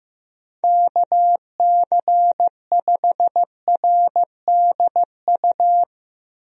Morse code, heard as a keyed tone, about 15 wpm, 700 Hz. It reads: KC5RDU
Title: USA Call Signs - 6 Characters - Repeated 3x - 15wpm